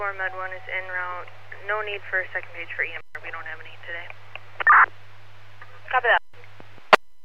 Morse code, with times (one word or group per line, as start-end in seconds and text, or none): none